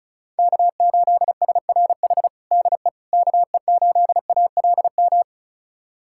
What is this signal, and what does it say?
Morse code, heard as a keyed tone, about 34 wpm, 700 Hz. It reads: K8SRH DE KE8ALM
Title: USA Callsign DE Callsign 35wpm